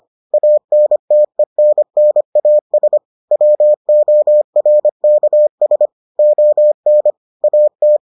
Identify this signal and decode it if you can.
Morse code, heard as a keyed tone, about 25 wpm, 600 Hz. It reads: ANTENNAS WORKS ON AT